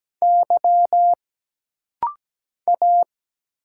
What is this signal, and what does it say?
Morse code, heard as a keyed tone, about 17 wpm, 700 Hz. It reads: Y A
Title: Single Letters - Rapid-Fire 17wpm